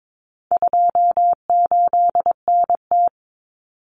Morse code, heard as a keyed tone, about 22 wpm, 700 Hz. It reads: 28NT
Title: Common Cut Numbers - 4-Digit 22wpm